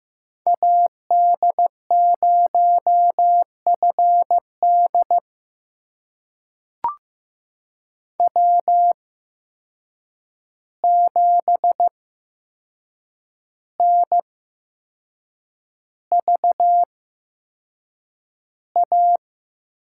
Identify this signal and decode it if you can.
Morse code, heard as a keyed tone, about 15 wpm, spaced as for 6 wpm, 700 Hz. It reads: AD0FD W7NVA
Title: USA Call Signs - 8x Character Spacing - 15wpm